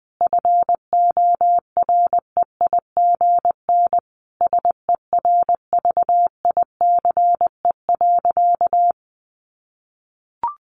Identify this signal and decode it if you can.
Morse code, heard as a keyed tone, about 20 wpm, 700 Hz. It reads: FOREIGN SERVICE.
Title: Sentences from Top 1400 Words 20wpm